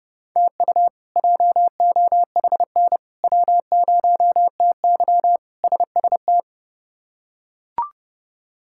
Morse code, heard as a keyed tone, about 32 wpm, 700 Hz. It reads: TU JOHN W0TY SST
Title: K1USN SST QSO Practice - 30wpm